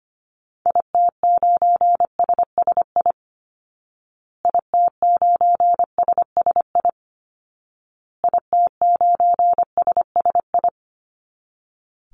Morse code, heard as a keyed tone, about 25 wpm, 700 Hz. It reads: IT9SSI IT9SSI IT9SSI
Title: Call Signs Repeated 2x - 25wpm